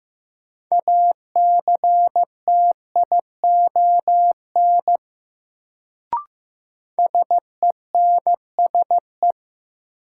Morse code, heard as a keyed tone, about 15 wpm, 700 Hz. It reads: ACTION SENSE
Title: Sentences from Top 700 Words - New Word Drill - 15wpm